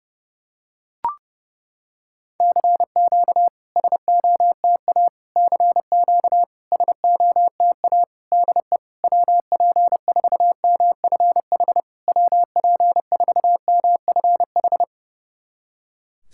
Morse code, heard as a keyed tone, about 30 wpm, 690 Hz. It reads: CQ SOTA CQ SOTA DE WP4MFH WP4MFH
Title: SOTA QSO Practice 30wpm